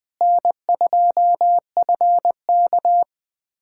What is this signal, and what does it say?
Morse code, heard as a keyed tone, about 20 wpm, 700 Hz. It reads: N2FK
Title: USA Callsign DE Callsign - 4 Characters - 20wpm